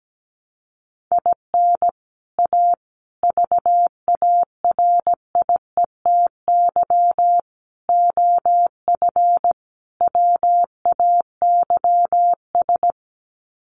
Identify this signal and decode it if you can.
Morse code, heard as a keyed tone, about 17 wpm, 700 Hz. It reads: IN A VARIETY OF WAYS
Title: Sets of 5 Words Encore 17wpm